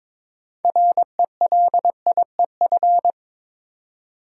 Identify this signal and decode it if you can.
Morse code, heard as a keyed tone, about 22 wpm, 700 Hz. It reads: RELIEF
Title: Six Letter Words - 22wpm